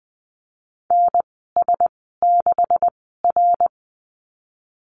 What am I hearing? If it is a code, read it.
Morse code, heard as a keyed tone, about 20 wpm, spaced as for 15 wpm, 700 Hz. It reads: NS6R
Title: 161 Lesson 26g - Letter g - Call Signs 20wpm